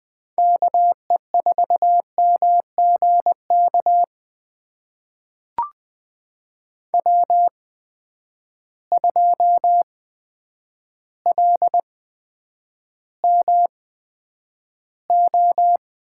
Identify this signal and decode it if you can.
Morse code, heard as a keyed tone, about 20 wpm, spaced as for 9 wpm, 700 Hz. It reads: KE4MGK W2LMO
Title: USA Call Signs - 8x Character Spacing - 20wpm